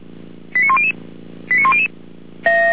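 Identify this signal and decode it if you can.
Five-tone selective call (ZVEI), heard as quick runs of five tones, tones of about 70 ms. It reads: 89100 89100
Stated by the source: ZVEI five-tone selective call received by SDR, archived by Feuerwehr Orlamünde (fire brigade).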